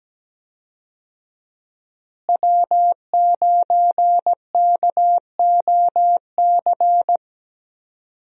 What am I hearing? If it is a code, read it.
Morse code, heard as a keyed tone, about 17 wpm, 700 Hz. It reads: W9KOC
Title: USA Call Signs - 5 Characters - Repeated 3x - 17wpm